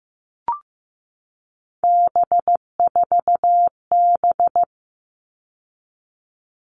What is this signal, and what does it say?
Morse code, heard as a keyed tone, about 15 wpm, 700 Hz. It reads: B4B
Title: All Cut Numbers - 3-Digit 15wpm